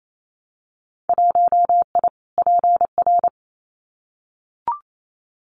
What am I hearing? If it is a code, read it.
Morse code, heard as a keyed tone, about 28 wpm, 700 Hz. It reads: 1I PR
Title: Winter Field Day Exchanges Encore 28wpm